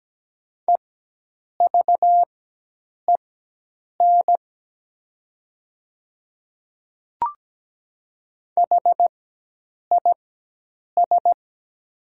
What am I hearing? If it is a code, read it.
Morse code, heard as a keyed tone, about 17 wpm, spaced as for 8 wpm, 700 Hz. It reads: EVEN HIS
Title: 4x Character Spacing - Top 200 Words - 17wpm